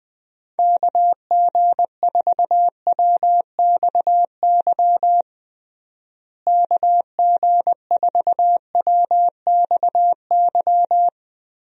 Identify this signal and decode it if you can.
Morse code, heard as a keyed tone, about 20 wpm, 700 Hz. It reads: KG4WXY KG4WXY
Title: USA Call Signs - 6 Characters - Repeated 3x - 20wpm